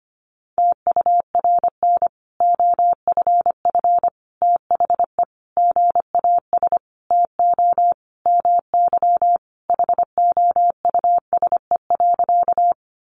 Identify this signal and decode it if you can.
Morse code, heard as a keyed tone, about 25 wpm, 700 Hz. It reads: TURN OFF THE GAS TO MY HOUSE.
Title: Sentences from Top 1100 Words 25wpm